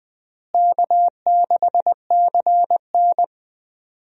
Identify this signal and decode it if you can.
Morse code, heard as a keyed tone, about 20 wpm, 700 Hz. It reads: K6CN
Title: USA Call Signs - 4 Characters - Repeated 3x - 20wpm